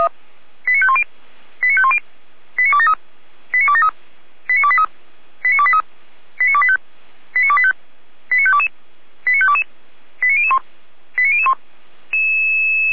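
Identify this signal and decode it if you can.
Five-tone selective call (ZVEI), heard as quick runs of five tones, tones of about 70 ms. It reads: 89510 89510 89273 89273 89283 89283 89285 89285 89522 89522 89001 89001
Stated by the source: ZVEI five-tone selective call received on an SDR.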